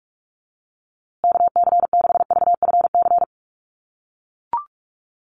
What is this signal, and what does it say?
Morse code, heard as a keyed tone, about 44 wpm, 700 Hz. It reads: KC6VFC
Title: USA Call Signs Repeated 2x 45wpm